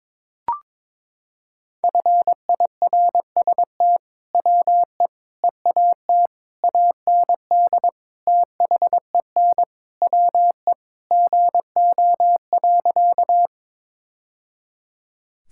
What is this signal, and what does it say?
Morse code, heard as a keyed tone, about 22 wpm, 700 Hz. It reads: FIRST WE EAT AND THEN WE GO.
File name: practice / 22wpm / Sentences with 2-Word Phrases - 22wpm